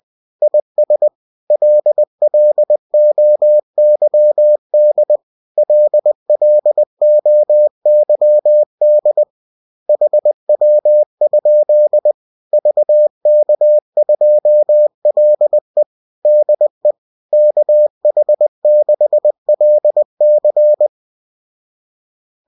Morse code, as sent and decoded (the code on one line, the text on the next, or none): .. ... / .-.. .-.. --- -.-- -.. / .-.. .-.. --- -.-- -.. / .... .-- ..--.. / ...- -.- ..--- .-.. / -.. / -.- .... -.... .-.. -.-.
IS LLOYD LLOYD HW? VK2LE DE KH6LC